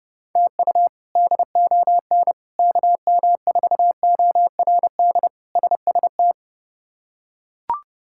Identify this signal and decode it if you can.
Morse code, heard as a keyed tone, about 29 wpm, 700 Hz. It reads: TU DON KM4ORD SST
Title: K1USN SST QSO Practice - 30wpm